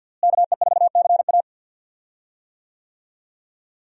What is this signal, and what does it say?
Morse code, heard as a keyed tone, about 52 wpm, 700 Hz. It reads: KE4CA
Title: USA Call Signs - 5 Characters - Repeated 2x - 50wpm